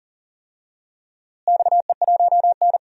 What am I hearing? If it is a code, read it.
Morse code, heard as a keyed tone, about 40 wpm, 700 Hz. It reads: XE1N
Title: DXCC-100 Call Signs - 4 Characters - Repeated 2x - 40wpm